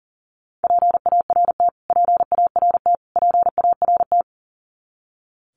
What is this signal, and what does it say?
Morse code, heard as a keyed tone, about 40 wpm, 700 Hz. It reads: PART PART PART
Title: Top 500 Words Repeated 40wpm